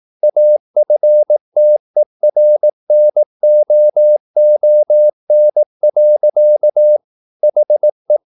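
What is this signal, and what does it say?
Morse code, heard as a keyed tone, about 18 wpm, 600 Hz. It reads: AFTERNOON. HE
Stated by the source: Morse practice at 18 wpm, English text, from CW med Gnister.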